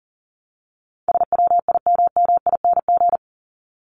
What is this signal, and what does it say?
Morse code, heard as a keyed tone, about 40 wpm, 700 Hz. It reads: SWIMMING
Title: Top 1501-1750 Words 40wpm